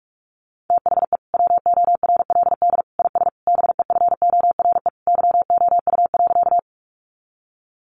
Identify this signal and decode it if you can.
Morse code, heard as a keyed tone, about 44 wpm, 700 Hz. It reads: THE WORLD IS BEFORE YOU.
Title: Sentences from Top 200 Words 45wpm